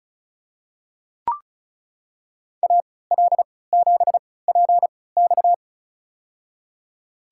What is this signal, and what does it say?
Morse code, heard as a keyed tone, about 34 wpm, spaced as for 20 wpm, 700 Hz. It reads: AL7PX
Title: USA Call Signs - 5 Characters - 3x Character Spacing - 35wpm